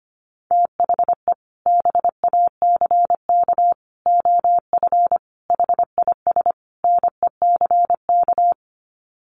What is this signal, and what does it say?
Morse code, heard as a keyed tone, about 25 wpm, 700 Hz. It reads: THE BACK OF HIS NECK
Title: Sets of 5 Words Encore 25wpm